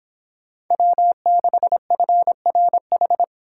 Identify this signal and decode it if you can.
Morse code, heard as a keyed tone, about 25 wpm, 700 Hz. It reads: W6FRH